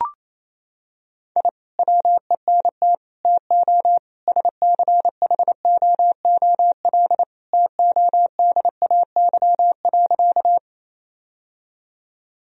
Morse code, heard as a keyed tone, about 28 wpm, 700 Hz. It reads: I WENT TO SCHOOL TODAY.
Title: Sentences from Top 1000 Words - Review Part 1 - 28wpm